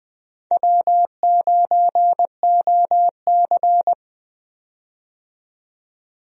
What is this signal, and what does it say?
Morse code, heard as a keyed tone, about 20 wpm, 700 Hz. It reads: W9OC
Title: USA Call Signs - 4 Characters - 20wpm